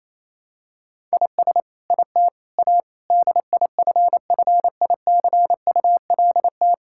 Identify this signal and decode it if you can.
Morse code, heard as a keyed tone, about 28 wpm, 700 Hz. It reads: IS IT A DIFFICULT
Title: Sentences from Top 1000 Words - Review Part 6 - 28wpm